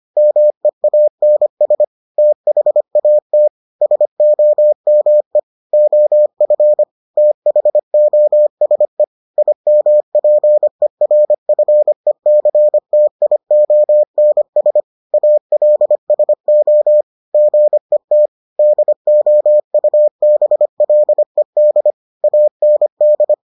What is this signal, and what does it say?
Morse code, heard as a keyed tone, about 25 wpm, 600 Hz. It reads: MEANS THAT SOME OF THOSE IMPERFECTIONS ALSO GET DOUBLED AND